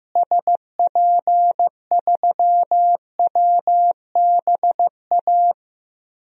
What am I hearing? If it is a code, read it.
Morse code, heard as a keyed tone, about 15 wpm, 700 Hz. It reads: SP3WBA
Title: DXCC-100 Call Signs - 6 Characters - Repeated 2x - 15wpm